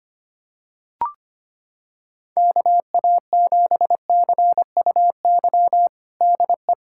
Morse code, heard as a keyed tone, about 25 wpm, 700 Hz. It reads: KA7CUY DE
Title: USA Callsign DE Callsign 25wpm